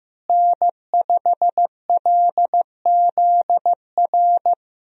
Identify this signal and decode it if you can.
Morse code, heard as a keyed tone, about 16 wpm, 700 Hz. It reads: N5LZR
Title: USA Call Signs - 5 Characters - Repeated 3x - 15wpm